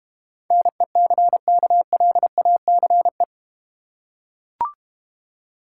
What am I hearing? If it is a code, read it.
Morse code, heard as a keyed tone, about 32 wpm, 700 Hz. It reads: NECKLACE